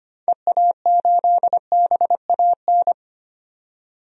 Morse code, heard as a keyed tone, about 25 wpm, 700 Hz. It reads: EA8BAN
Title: DXCC-100 Call Signs - 6 Characters - 25wpm